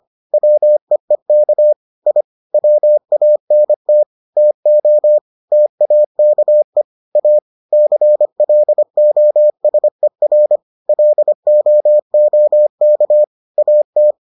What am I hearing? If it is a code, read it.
Morse code, heard as a keyed tone, about 25 wpm, 600 Hz. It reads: WEEK I WANT TO TAKE A CLOSER LOOK AT